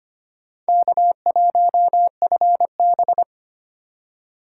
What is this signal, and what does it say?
Morse code, heard as a keyed tone, about 33 wpm, 700 Hz. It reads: K1FB